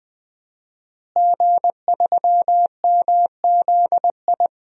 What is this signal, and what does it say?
Morse code, heard as a keyed tone, about 20 wpm, 700 Hz. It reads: G3MZI
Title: DXCC-100 Call Signs - 5 Characters - Repeated 3x - 20wpm